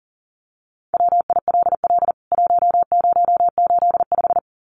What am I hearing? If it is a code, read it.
Morse code, heard as a keyed tone, about 40 wpm, 700 Hz. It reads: WILL 1085
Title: CWT Exchange 40wpm